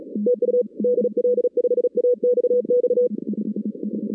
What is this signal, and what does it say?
Morse code, heard as a keyed tone, about 37 wpm, 490 Hz. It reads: TU DL5AXX
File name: 2016 arrl dx cw / 2016-ARRL-DX-CW-K5TR- - 0-41